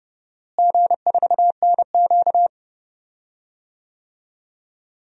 Morse code, heard as a keyed tone, about 30 wpm, 700 Hz. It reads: G4NQ
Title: DXCC-100 Call Signs - 4 Characters - Repeated 2x - 30wpm